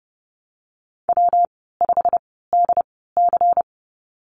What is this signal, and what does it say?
Morse code, heard as a keyed tone, about 30 wpm, 700 Hz. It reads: W 5 D C
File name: Lesson 19g - Letter w - Call Signs 30wpm